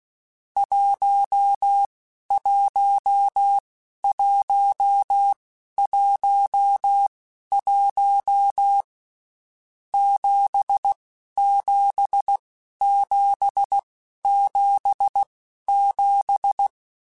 Morse code, heard as a keyed tone, about 16 wpm, spaced as for 11 wpm, 800 Hz. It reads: 11111 77777
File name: Basic Sets / Basic Letter Groups Collective7 / JZ17@10